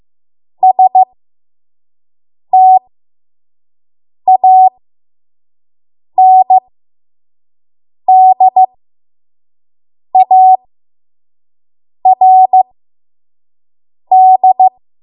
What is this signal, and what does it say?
Morse code, heard as a keyed tone, about 15 wpm, spaced as for 5 wpm, 750 Hz. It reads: STANDARD
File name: Morse / 05WPM